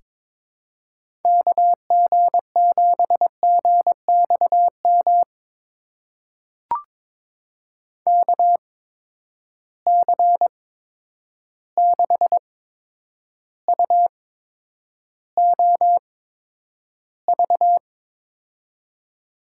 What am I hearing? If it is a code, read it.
Morse code, heard as a keyed tone, about 22 wpm, spaced as for 9 wpm, 700 Hz. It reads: KG7GXM KC6UOV
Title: USA Call Signs - 6 Characters - 8x Character Spacing - 22wpm